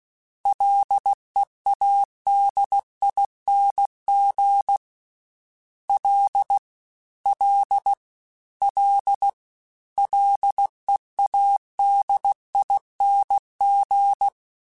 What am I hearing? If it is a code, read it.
Morse code, heard as a keyed tone, about 16 wpm, 800 Hz. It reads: LEADING L L L LEADING
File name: Prefixes 3x2 Twice @15